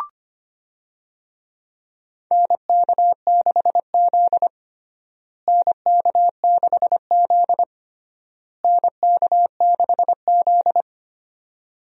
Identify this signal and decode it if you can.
Morse code, heard as a keyed tone, about 25 wpm, 700 Hz. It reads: NK6Z NK6Z NK6Z
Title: USA Call Signs - 4 Characters - Repeated 3x - 25wpm